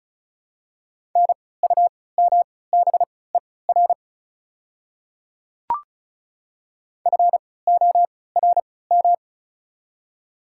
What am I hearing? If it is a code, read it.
Morse code, heard as a keyed tone, about 36 wpm, spaced as for 20 wpm, 700 Hz. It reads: NUMBER FORM
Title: 3x Character Spacing - Top 300 Words - 35wpm